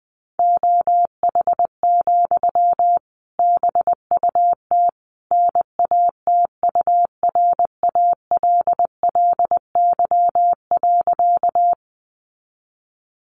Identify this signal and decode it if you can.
Morse code, heard as a keyed tone, about 20 wpm, 700 Hz. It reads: OH, BUT NATURALLY.
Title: Sentences from Top 1300 Words 20wpm